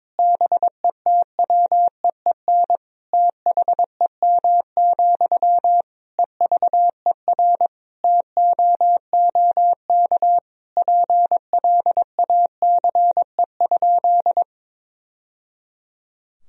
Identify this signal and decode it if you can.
Morse code, heard as a keyed tone, about 22 wpm, 700 Hz. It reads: BETWEEN THEM, EVER TOOK PLACE?
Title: Sentences from Top 1000 Words - Review Part 1 - 22wpm